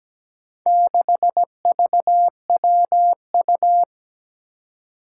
Morse code, heard as a keyed tone, about 17 wpm, 700 Hz. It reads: 6VWU
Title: Four Challenging Letter-Number Combinations 17wpm